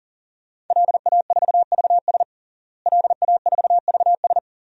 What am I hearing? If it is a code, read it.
Morse code, heard as a keyed tone, about 40 wpm, 700 Hz. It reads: LA4VS LA4VS
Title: DXCC-100 Call Signs - 5 Characters - Repeated 2x - 40wpm